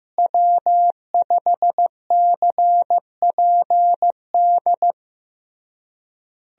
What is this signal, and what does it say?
Morse code, heard as a keyed tone, about 16 wpm, 700 Hz. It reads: W5CPD
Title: USA Call Signs - 5 Characters - Repeated 3x - 15wpm